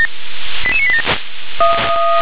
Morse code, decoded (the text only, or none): none